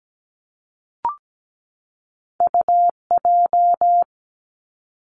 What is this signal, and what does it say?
Morse code, heard as a keyed tone, about 17 wpm, 700 Hz. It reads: UJ
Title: Two Challenging Letter-Number Combinations 17wpm